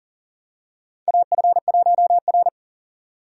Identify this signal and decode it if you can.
Morse code, heard as a keyed tone, about 40 wpm, 700 Hz. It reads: AF1R
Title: USA Call Signs - 4 Characters - 2x Character Spacing - 40wpm